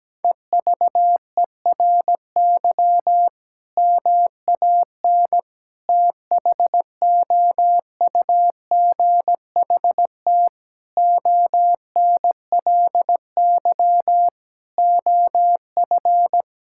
Morse code, heard as a keyed tone, about 17 wpm, 700 Hz. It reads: EVERY MAN THOUGHT ONLY OF
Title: Sentences from Top 1600 Words 17wpm